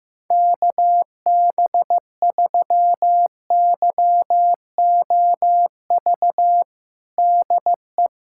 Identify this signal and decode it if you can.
Morse code, heard as a keyed tone, about 15 wpm, 700 Hz. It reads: KB3YOV DE
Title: USA Callsign DE Callsign - 6 Characters - 15wpm